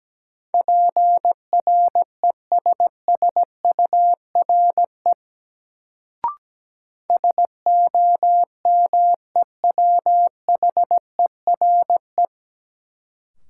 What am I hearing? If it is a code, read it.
Morse code, heard as a keyed tone, about 17 wpm, 700 Hz. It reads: PRESSURE SOMEWHERE
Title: Sentences from Top 1400 Words - New Word Drill Alt - 17wpm